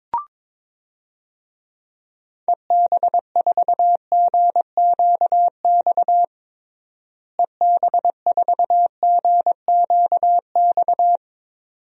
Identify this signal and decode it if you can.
Morse code, heard as a keyed tone, about 22 wpm, 700 Hz. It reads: EB4GQX EB4GQX
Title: DXCC-100 Call Signs - 6 Characters - Repeated 2x - 22wpm